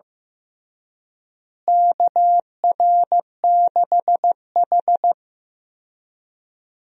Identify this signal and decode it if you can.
Morse code, heard as a keyed tone, about 15 wpm, 700 Hz. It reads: KR6H
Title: USA Call Signs - 4 Characters - Repeated 2x - 15wpm